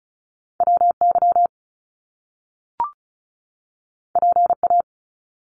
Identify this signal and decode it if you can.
Morse code, heard as a keyed tone, about 36 wpm, 700 Hz. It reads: WY PA
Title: US States 35wpm